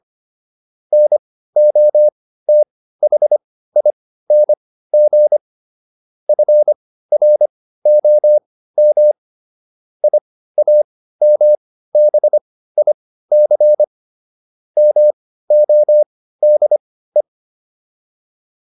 Morse code, heard as a keyed tone, about 25 wpm, spaced as for 15 wpm, 600 Hz. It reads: NOTHING FROM IAMBIC MODE